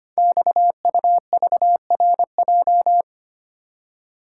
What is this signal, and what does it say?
Morse code, heard as a keyed tone, about 25 wpm, 700 Hz. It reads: XUVRJ
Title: Code Groups - 5 Letters - 25wpm